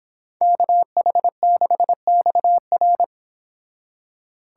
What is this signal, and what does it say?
Morse code, heard as a keyed tone, about 30 wpm, 700 Hz. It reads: KH6XR